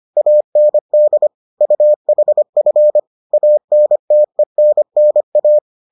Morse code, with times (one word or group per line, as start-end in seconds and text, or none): 0.2-1.3 AND
1.6-3.0 UHF
3.3-5.6 ANTENNA